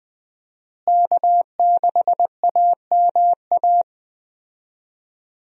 Morse code, heard as a keyed tone, about 21 wpm, 700 Hz. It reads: K6AMA
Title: USA Call Signs - 5 Characters - Repeated 2x - 20wpm